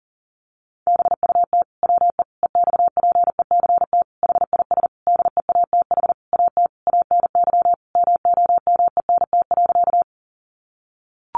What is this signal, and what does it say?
Morse code, heard as a keyed tone, about 40 wpm, 700 Hz. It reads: BUT WE EXPECT HIS DEATH AT ANY MOMENT.
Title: Sentences from Top 1100 Words 40wpm